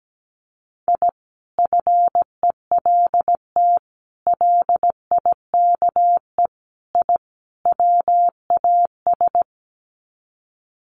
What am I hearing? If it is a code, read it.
Morse code, heard as a keyed tone, about 17 wpm, 700 Hz. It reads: I FELT LIKE I WAS
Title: Sets of 5 Words Encore 17wpm